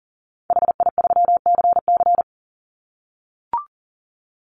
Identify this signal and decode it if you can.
Morse code, heard as a keyed tone, about 40 wpm, 700 Hz. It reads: HI3CC
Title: Call Signs - 40wpm